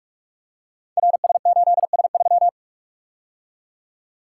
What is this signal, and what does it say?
Morse code, heard as a keyed tone, about 44 wpm, 700 Hz. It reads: RS8S3
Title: Code Groups - 5 Letter:Numbers - 45wpm